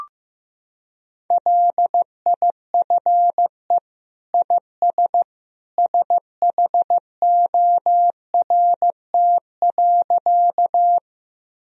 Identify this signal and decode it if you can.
Morse code, heard as a keyed tone, about 15 wpm, 700 Hz. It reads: LIFE IS SHORT.
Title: Sentences from Top 1000 Words - Review Part 2 - 15wpm